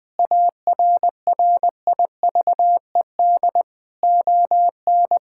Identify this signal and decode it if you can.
Morse code, heard as a keyed tone, about 20 wpm, 700 Hz. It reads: ARRIVED ON
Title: Sentences with 2-Word Phrases - Spoken Beginning - 20wpm